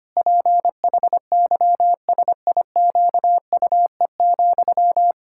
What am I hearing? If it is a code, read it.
Morse code, heard as a keyed tone, about 25 wpm, 700 Hz. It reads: PHYSIQUE,
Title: Pangram Challenge v2 - 25wpm